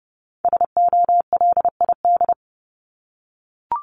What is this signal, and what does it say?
Morse code, heard as a keyed tone, about 30 wpm, 700 Hz. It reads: SOLID
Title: Top 1251-1500 Words 30wpm